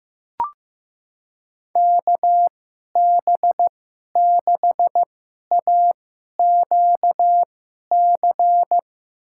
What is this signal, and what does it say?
Morse code, heard as a keyed tone, about 15 wpm, spaced as for 11 wpm, 700 Hz. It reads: KB6AQC
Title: USA Call Signs - 2x Character Spacing - 15wpm